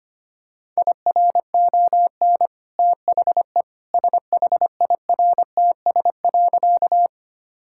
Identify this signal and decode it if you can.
Morse code, heard as a keyed tone, about 25 wpm, 690 Hz. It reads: IRON THE SHIRTS.
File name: Easy Sentences with Hints - 25wpm